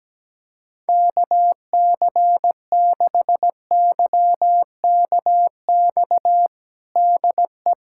Morse code, heard as a keyed tone, about 17 wpm, 700 Hz. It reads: KC6YKX DE
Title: USA Callsign DE Callsign - 6 Characters - 17wpm